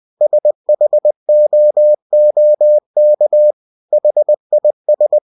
Morse code, heard as a keyed tone, about 20 wpm, 600 Hz. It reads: SHOOK HIS
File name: cw-books / winnie-the-pooh / winnie-20wpm-page-0026